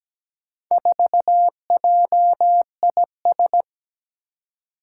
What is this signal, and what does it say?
Morse code, heard as a keyed tone, about 17 wpm, 700 Hz. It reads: 4JIS
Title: Four Challenging Letter-Number Combinations 17wpm